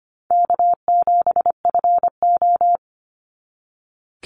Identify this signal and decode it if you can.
Morse code, heard as a keyed tone, about 25 wpm, 700 Hz. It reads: K7FO
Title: NA SOTA Chaser Callsigns 25wpm